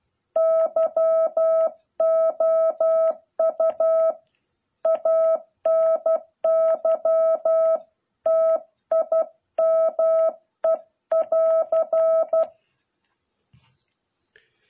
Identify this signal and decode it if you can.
Morse code, heard as a keyed tone, about 12 wpm, 660 Hz. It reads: YOU ANY TIME <AR>